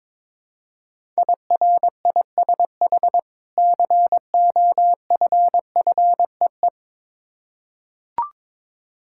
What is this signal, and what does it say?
Morse code, heard as a keyed tone, about 22 wpm, 690 Hz. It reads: IRISH COFFEE
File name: Difficult Words with Hints - 22wpm